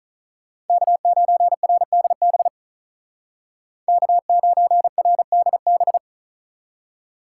Morse code, heard as a keyed tone, about 46 wpm, 700 Hz. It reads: K9RDB K9RDB